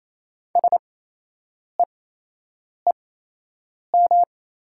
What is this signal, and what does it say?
Morse code, heard as a keyed tone, about 28 wpm, spaced as for 8 wpm, 700 Hz. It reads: SEEM
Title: 8x Character Spacing - Top 500 Words - 28wpm